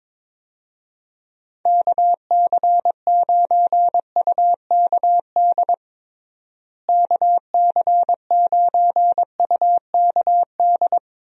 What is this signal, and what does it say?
Morse code, heard as a keyed tone, about 21 wpm, 700 Hz. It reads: KC9UKD KC9UKD
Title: USA Call Signs - 6 Characters - Repeated 2x - 22wpm